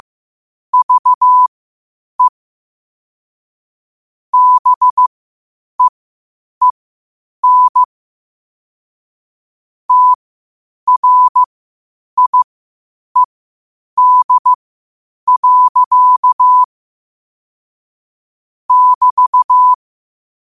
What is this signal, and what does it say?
Morse code, heard as a keyed tone, about 15 wpm, spaced as for 8 wpm, 1000 Hz. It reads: VE BEEN TRIED. <BT>
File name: qotdfiles / 8wpm